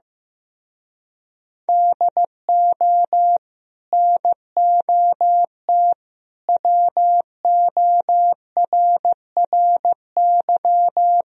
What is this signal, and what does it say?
Morse code, heard as a keyed tone, about 15 wpm, 700 Hz. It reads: DO NOT WORRY
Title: Speed-Racing - Sets of 5 Words 15wpm